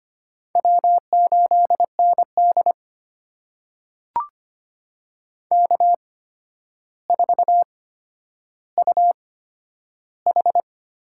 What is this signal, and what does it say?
Morse code, heard as a keyed tone, about 25 wpm, spaced as for 11 wpm, 700 Hz. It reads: W8ND K4UH
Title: USA Call Signs - 4 Characters - 8x Character Spacing - 25wpm